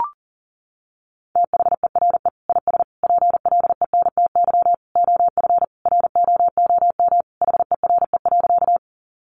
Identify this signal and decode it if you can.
Morse code, heard as a keyed tone, about 40 wpm, 700 Hz. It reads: THERE IS PLENTY OF ROOM HERE.
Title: Sentences from Top 1300 Words 40wpm